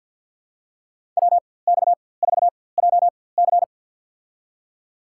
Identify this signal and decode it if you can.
Morse code, heard as a keyed tone, about 46 wpm, spaced as for 23 wpm, 700 Hz. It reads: WX4JC
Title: USA Call Signs - 5 Characters - 4x Character Spacing - 50wpm